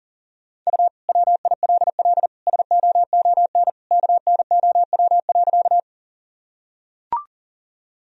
Morse code, heard as a keyed tone, about 40 wpm, 700 Hz. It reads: U WILL SOON KNOW.
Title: Sentences from Top 400 Words and Abbreviations 40wpm